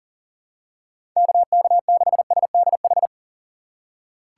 Morse code, heard as a keyed tone, about 40 wpm, 700 Hz. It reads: KK6SDH
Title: USA Call Signs - 6 Characters - 4x Character Spacing - 40wpm